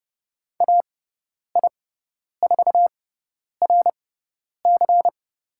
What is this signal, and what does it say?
Morse code, heard as a keyed tone, about 30 wpm, spaced as for 10 wpm, 700 Hz. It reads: AI4RC
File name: Lesson 18g - Letter m - Call Signs 30wpm